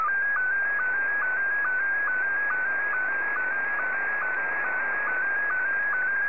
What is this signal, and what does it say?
SSTV (slow-scan television), Scottie 1.
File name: sstv / S1-VK2OY